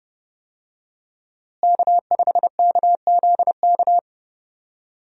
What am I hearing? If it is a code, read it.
Morse code, heard as a keyed tone, about 28 wpm, 700 Hz. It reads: K5KZK
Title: USA Call Signs - 5 Characters - Repeated 3x - 30wpm